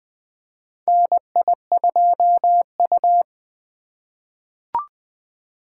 Morse code, heard as a keyed tone, about 20 wpm, 700 Hz. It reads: NI2U
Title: USA Call Signs - 2x Character Spacing - 20wpm